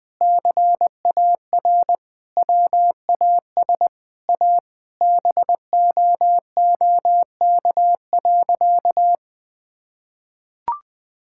Morse code, heard as a keyed tone, about 20 wpm, 700 Hz. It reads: CAR WAS A BOOK.
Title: Sentences with 4-Word Phrases - Spoken Beginning - 20wpm